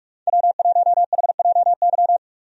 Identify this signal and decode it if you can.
Morse code, heard as a keyed tone, about 42 wpm, 700 Hz. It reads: W1HJY
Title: USA Call Signs - 5 Characters - Repeated 2x - 45wpm